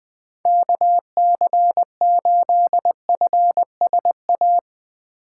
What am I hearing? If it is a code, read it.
Morse code, heard as a keyed tone, about 20 wpm, 700 Hz. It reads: KC8FSA
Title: USA Call Signs - 4x Character Spacing - 20wpm